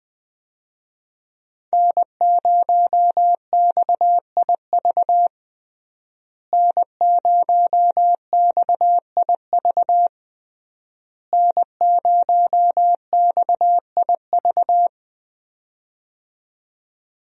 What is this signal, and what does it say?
Morse code, heard as a keyed tone, about 20 wpm, 700 Hz. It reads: N0XIV N0XIV N0XIV
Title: USA Call Signs - 5 Characters - Repeated 3x - 20wpm